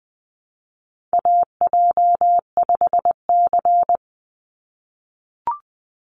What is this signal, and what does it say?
Morse code, heard as a keyed tone, about 20 wpm, 700 Hz. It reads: AJ5C
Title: NA SOTA Chaser Callsigns 20wpm